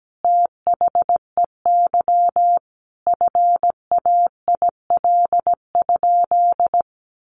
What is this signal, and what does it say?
Morse code, heard as a keyed tone, about 17 wpm, 700 Hz. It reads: THEY FAIL?
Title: Sentences from Top 1100 Words 17wpm